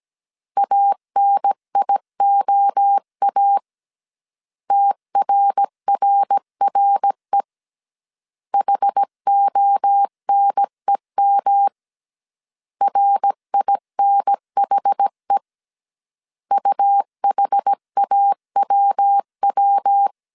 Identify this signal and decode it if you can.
Morse code, heard as a keyed tone, about 17 wpm, 800 Hz. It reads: ANIOA TRRRE HONEM RINHE UHAWW
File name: CW / DET / 1 3 1500C S A 16wpm 2141